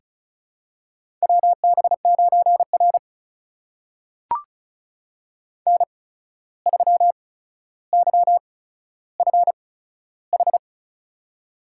Morse code, heard as a keyed tone, about 36 wpm, spaced as for 14 wpm, 700 Hz. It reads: WB9R N3YFH
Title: USA Call Signs - 8x Character Spacing - 35wpm